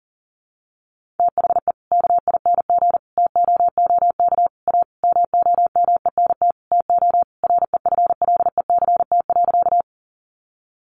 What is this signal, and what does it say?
Morse code, heard as a keyed tone, about 40 wpm, 700 Hz. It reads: THE KING TOOK A MOMENT TO REFLECT.
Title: Sentences from Top 1500 Words 40wpm